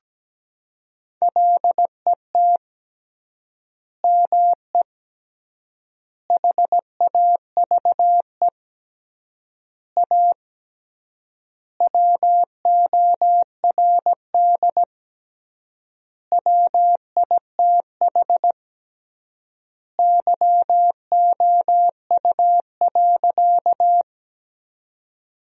Morse code, heard as a keyed tone, about 17 wpm, 700 Hz. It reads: LET ME HAVE A WORD WITH YOU.
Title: 3x Word-Spacing - Sentences from Top 300 Words - 17wpm